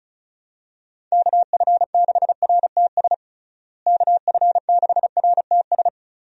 Morse code, heard as a keyed tone, about 36 wpm, 700 Hz. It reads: KF6RTS KF6RTS
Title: USA Call Signs - 6 Characters - Repeated 2x - 35wpm